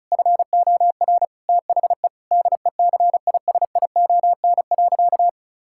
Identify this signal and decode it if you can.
Morse code, heard as a keyed tone, about 36 wpm, 700 Hz. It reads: FOR THE DECISION.
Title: Sentences from Top 1600 Words 35wpm